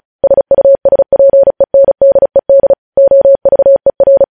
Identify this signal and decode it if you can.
Morse code, heard as a keyed tone, about 36 wpm, 550 Hz. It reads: SUSPENDED OVER
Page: news headlines in morse code at 35 wpm